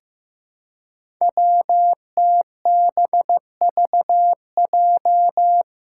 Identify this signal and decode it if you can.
Morse code, heard as a keyed tone, about 15 wpm, 700 Hz. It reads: WTBVJ
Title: Five Challenging Letter-Number Combinations 15wpm